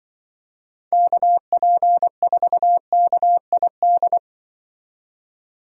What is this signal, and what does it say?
Morse code, heard as a keyed tone, about 25 wpm, 700 Hz. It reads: KP4KID